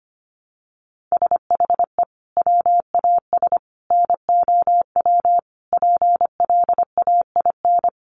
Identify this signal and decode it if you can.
Morse code, heard as a keyed tone, about 25 wpm, 700 Hz. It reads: SHE WAS NOW PLAIN
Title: Sentences from Top 1000 Words 25wpm